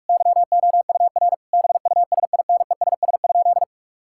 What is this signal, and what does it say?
Morse code, heard as a keyed tone, about 46 wpm, 700 Hz. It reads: YOUR BUSINESS?
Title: Sentences from Top 1000 Words - Review Part 6 - 45wpm